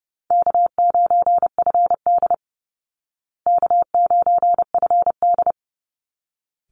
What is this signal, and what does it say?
Morse code, heard as a keyed tone, about 30 wpm, 700 Hz. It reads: K9FD K9FD
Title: Call Signs Repeated 3x - 30wpm